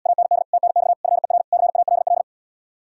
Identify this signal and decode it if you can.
Morse code, heard as a keyed tone, about 80 wpm, 700 Hz. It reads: AND TNX FER CALL